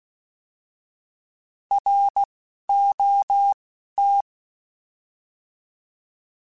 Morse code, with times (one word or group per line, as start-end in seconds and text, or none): 1.7-2.2 R
2.7-3.5 O
4.0-4.2 T